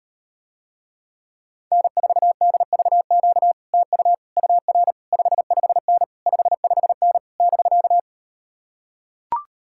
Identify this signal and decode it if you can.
Morse code, heard as a keyed tone, about 39 wpm, 700 Hz. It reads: N4DVQ TU UR 55N 55N <BK>